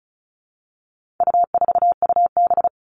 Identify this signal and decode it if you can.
Morse code, heard as a keyed tone, about 34 wpm, 700 Hz. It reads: U4UB
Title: All Cut Numbers - 4-Digit 35wpm